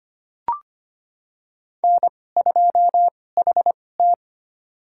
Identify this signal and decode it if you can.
Morse code, heard as a keyed tone, about 25 wpm, spaced as for 18 wpm, 700 Hz. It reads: N2HT
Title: USA Call Signs - 4 Characters - 2x Character Spacing - 25wpm